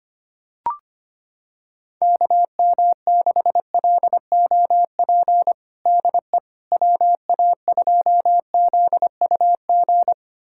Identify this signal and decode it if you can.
Morse code, heard as a keyed tone, about 25 wpm, 700 Hz. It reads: KM6LOP DE WA2ZUG
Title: USA Callsign DE Callsign - 6 Characters - 25wpm